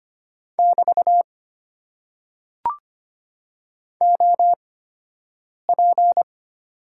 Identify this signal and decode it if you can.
Morse code, heard as a keyed tone, about 25 wpm, spaced as for 7 wpm, 700 Hz. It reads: <BT> OP
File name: Top 90 QSO Elements - 8x Character Spacing - 25wpm